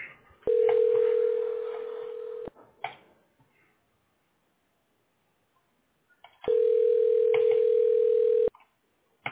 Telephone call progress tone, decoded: ringback tone